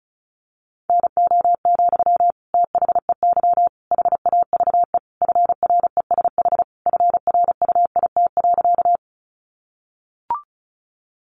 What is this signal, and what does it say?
Morse code, heard as a keyed tone, about 34 wpm, 700 Hz. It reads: NO, THEY HAVE FRESH FRUIT.
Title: Sentences from Top 1300 Words 35wpm